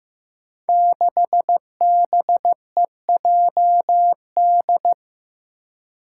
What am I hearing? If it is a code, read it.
Morse code, heard as a keyed tone, about 15 wpm, 700 Hz. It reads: 6BEJD
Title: Five Challenging Letter-Number Combinations 15wpm